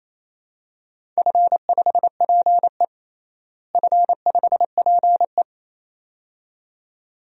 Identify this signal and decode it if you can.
Morse code, heard as a keyed tone, about 27 wpm, 700 Hz. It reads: F5PE F5PE
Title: DXCC-100 Call Signs - 4 Characters - Repeated 2x - 28wpm